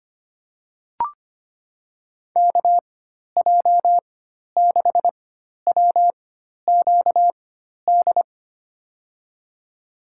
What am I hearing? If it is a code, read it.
Morse code, heard as a keyed tone, about 25 wpm, spaced as for 12 wpm, 700 Hz. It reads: KJ6WQD
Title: USA Call Signs - 6 Characters - 4x Character Spacing - 25wpm